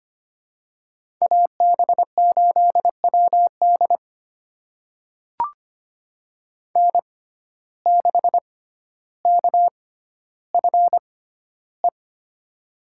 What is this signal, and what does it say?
Morse code, heard as a keyed tone, about 25 wpm, spaced as for 13 wpm, 700 Hz. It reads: AB8WD N6KFE
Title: USA Call Signs - 6x Character Spacing - 25wpm